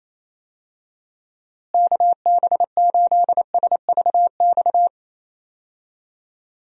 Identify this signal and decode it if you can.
Morse code, heard as a keyed tone, about 28 wpm, 700 Hz. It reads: KB8SVX
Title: USA Call Signs - 6 Characters - Repeated 2x - 28wpm